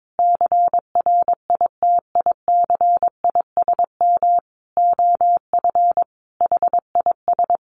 Morse code, heard as a keyed tone, about 22 wpm, 700 Hz. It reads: CRITICISM OF HIS
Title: Sentences from Top 1300 Words 22wpm